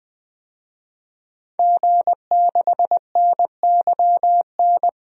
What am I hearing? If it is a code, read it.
Morse code, heard as a keyed tone, about 20 wpm, 700 Hz. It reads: G6NYN
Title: DXCC-100 Call Signs - 5 Characters - 20wpm